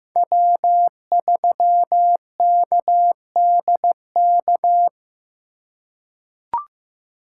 Morse code, heard as a keyed tone, about 15 wpm, 700 Hz. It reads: W3KDK
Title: USA Call Signs - 5 Characters - 2x Character Spacing - 15wpm